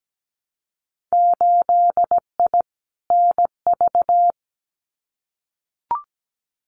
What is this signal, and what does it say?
Morse code, heard as a keyed tone, about 17 wpm, 700 Hz. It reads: 8I NV
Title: Winter Field Day Exchanges Encore 17wpm